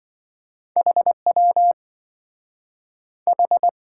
Morse code, heard as a keyed tone, about 27 wpm, 700 Hz. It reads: HW H